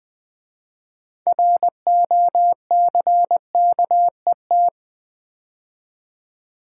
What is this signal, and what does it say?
Morse code, heard as a keyed tone, about 20 wpm, 700 Hz. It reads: ROCKET
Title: 6 Letter Words - First 3 Letters Given - 20wpm